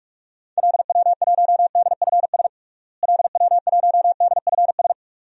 Morse code, heard as a keyed tone, about 41 wpm, 700 Hz. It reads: LW1DFS LW1DFS
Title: DXCC-100 Call Signs - 6 Characters - Repeated 2x - 45wpm